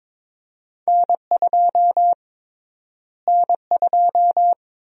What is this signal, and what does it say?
Morse code, heard as a keyed tone, about 21 wpm, 700 Hz. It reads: N2 N2
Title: USA Call Sign Prefixes - Repeated 2x - 22wpm